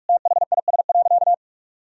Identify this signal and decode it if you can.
Morse code, heard as a keyed tone, about 46 wpm, 700 Hz. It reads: THIS.
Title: Sentences from Top 1000 Words - Review Part 1 - 45wpm